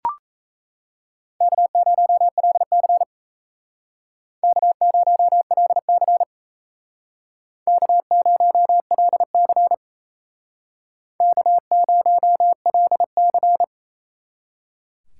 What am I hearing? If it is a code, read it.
Morse code, heard as a keyed tone, about 35 wpm, 700 Hz. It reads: K0LC K0LC K0LC K0LC